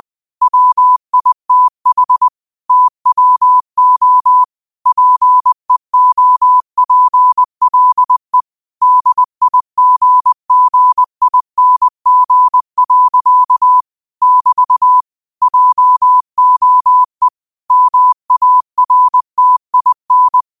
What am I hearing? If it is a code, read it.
Morse code, heard as a keyed tone, about 20 wpm, 1000 Hz. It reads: WITH TWO PEOPLE DIGGING. <BT> JOE MARTIN